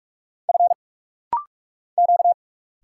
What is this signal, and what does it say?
Morse code, heard as a keyed tone, about 46 wpm, 700 Hz. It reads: F Q
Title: Single Letter-Number Alt - Rapid-Fire 45wpm